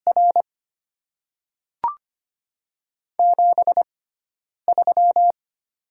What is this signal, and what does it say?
Morse code, heard as a keyed tone, about 25 wpm, spaced as for 9 wpm, 690 Hz. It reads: R 73
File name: Top 20 QSO Elements - 6x Character Spacing - 25wpm